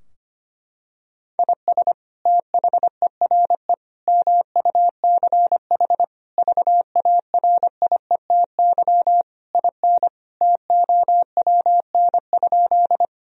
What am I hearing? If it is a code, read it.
Morse code, heard as a keyed tone, about 25 wpm, 700 Hz. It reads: IS THERE MUCH VARIETY IN TOWN?
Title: Sentences from Top 1000 Words - Review Part 4 - 25wpm